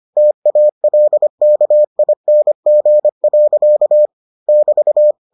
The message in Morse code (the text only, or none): TALKING. <BT>